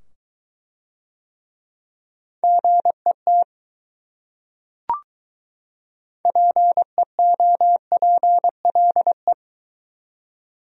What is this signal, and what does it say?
Morse code, heard as a keyed tone, about 22 wpm, 700 Hz. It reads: GET PEOPLE